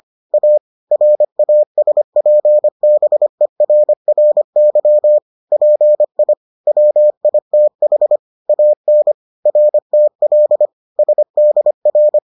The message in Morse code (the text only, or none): A RASPBERRY PI WITH AN RTL SDR